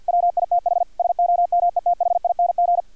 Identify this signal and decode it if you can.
Morse code, heard as a keyed tone, about 50 wpm, 700 Hz, with radio noise. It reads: WITH SOMETHING
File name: archived / CRSnet